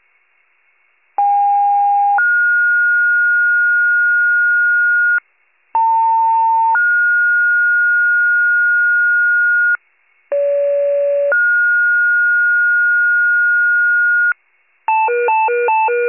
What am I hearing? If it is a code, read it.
Two-tone page, heard as a one-second tone, then a three-second tone, about 810 then 1430 Hz, about 900 then 1430 Hz, about 570 then 1430 Hz. It is heard from 1.2 to 14.3 s.
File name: pager / 78 FIRE / wcalert